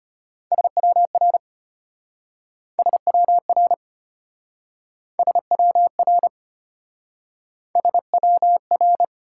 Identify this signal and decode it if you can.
Morse code, heard as a keyed tone, about 33 wpm, 700 Hz. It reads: SWR SWR SWR SWR